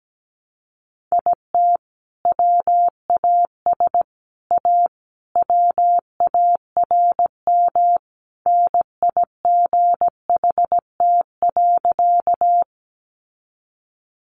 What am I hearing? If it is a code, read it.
Morse code, heard as a keyed tone, about 17 wpm, 700 Hz. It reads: IT WAS A WARM NIGHT.
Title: Sentences from Top 500 Words 17wpm